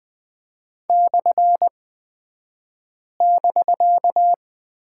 Morse code, heard as a keyed tone, about 20 wpm, spaced as for 5 wpm, 700 Hz. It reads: /<BK>
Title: 249 Lesson 40d - BK - Two Characters 20wpm